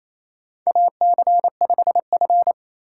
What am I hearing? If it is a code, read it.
Morse code, heard as a keyed tone, about 27 wpm, 700 Hz. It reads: AC5F
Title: USA Call Signs - 4 Characters - 3x Character Spacing - 28wpm